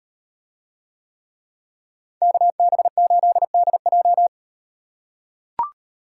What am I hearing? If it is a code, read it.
Morse code, heard as a keyed tone, about 36 wpm, 700 Hz. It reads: KB8DJ